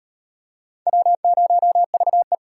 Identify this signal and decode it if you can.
Morse code, heard as a keyed tone, about 34 wpm, 700 Hz. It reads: W0VE